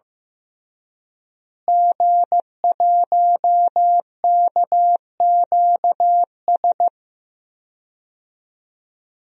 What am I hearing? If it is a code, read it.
Morse code, heard as a keyed tone, about 15 wpm, 700 Hz. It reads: G1KQS
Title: DXCC-100 Call Signs - 5 Characters - Repeated 2x - 15wpm